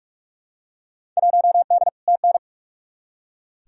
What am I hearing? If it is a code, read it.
Morse code, heard as a keyed tone, about 46 wpm, 700 Hz. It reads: 1D TN
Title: Field Day Exchanges - 45wpm